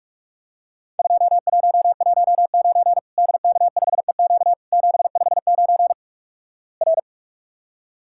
Morse code, heard as a keyed tone, about 45 wpm, 700 Hz. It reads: 2119 DK5EQ 759 R
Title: Worked All Europe DX Contest - QTC Exchange - 45wpm